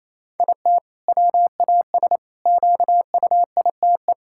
Morse code, heard as a keyed tone, about 28 wpm, 700 Hz. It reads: IT WAS QUITE
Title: Sentences from Top 1000 Words - Review Part 5 - 28wpm